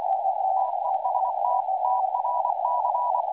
Morse code, heard as a keyed tone, about 24 wpm, 970 Hz, with radio noise. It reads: TEST TRC
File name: attachments / websdr recording 7029.0kHz